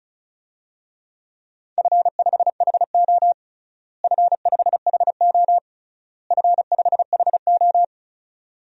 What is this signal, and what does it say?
Morse code, heard as a keyed tone, about 36 wpm, 700 Hz. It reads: F5HO F5HO F5HO
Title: DXCC-100 Call Signs - 4 Characters - Repeated 3x - 35wpm